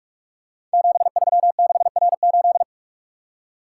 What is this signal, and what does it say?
Morse code, heard as a keyed tone, about 44 wpm, 690 Hz. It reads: 736R8
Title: Code Groups - 5 Letter:Numbers - 45wpm